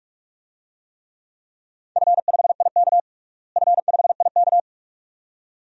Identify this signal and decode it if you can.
Morse code, heard as a keyed tone, about 46 wpm, 700 Hz. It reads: F5IK F5IK
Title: DXCC-100 Call Signs - 4 Characters - Repeated 2x - 45wpm